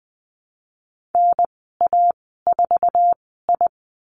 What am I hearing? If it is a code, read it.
Morse code, heard as a keyed tone, about 20 wpm, spaced as for 14 wpm, 700 Hz. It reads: NA4I
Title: 055 Lesson 10g - Letter r - Call Signs 20wpm